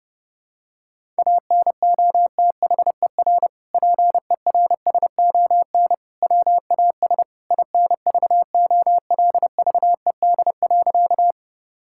Morse code, heard as a keyed tone, about 32 wpm, 700 Hz. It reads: ANOTHER PERSON WAS INVOLVED.
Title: Sentences from Top 1000 Words - Review Part 5 - 30wpm